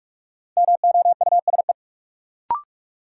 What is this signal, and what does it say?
Morse code, heard as a keyed tone, about 41 wpm, 700 Hz. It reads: MOUSE
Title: Sentences from Top 700 Words - New Word Drill - 45wpm